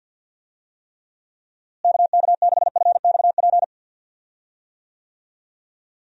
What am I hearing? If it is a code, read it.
Morse code, heard as a keyed tone, about 51 wpm, 700 Hz. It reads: KK6FXP
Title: USA Call Signs - 6 Characters - Repeated 2x - 50wpm